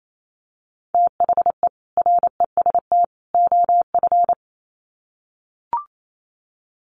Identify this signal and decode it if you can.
Morse code, heard as a keyed tone, about 28 wpm, 700 Hz. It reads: THE REST OF
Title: Sets of 3 Words 28wpm